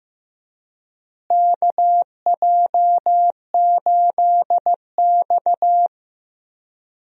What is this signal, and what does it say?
Morse code, heard as a keyed tone, about 15 wpm, 700 Hz. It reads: KJ8X
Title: USA Call Signs - 4 Characters - Repeated 3x - 15wpm